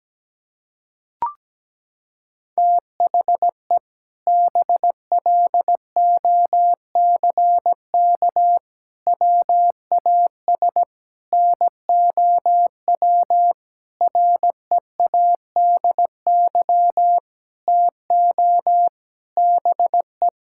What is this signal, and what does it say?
Morse code, heard as a keyed tone, about 17 wpm, 700 Hz. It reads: THE BLOCK WAS NOW READY TO BE
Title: Sentences from Top 1600 Words 17wpm